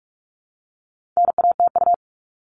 Morse code, heard as a keyed tone, about 44 wpm, 700 Hz. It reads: NATU
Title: All Cut Numbers - 4-Digit 45wpm